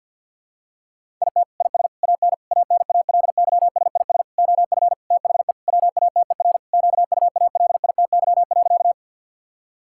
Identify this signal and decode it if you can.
Morse code, heard as a keyed tone, about 52 wpm, 700 Hz. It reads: IT IS AN ANALYSIS OF THE WATER QUALITY.
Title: Sentences from Top 1000 Words - Review Part 5 - 50wpm